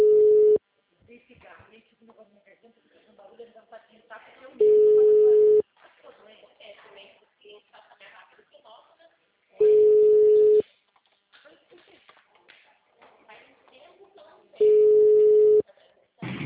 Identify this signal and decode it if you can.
Ringback tone (Europe, 425 Hz), with speech.